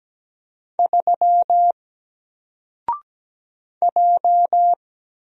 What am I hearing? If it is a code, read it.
Morse code, heard as a keyed tone, about 17 wpm, 700 Hz. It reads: 3 J
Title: Single Letter-Number Alt 17wpm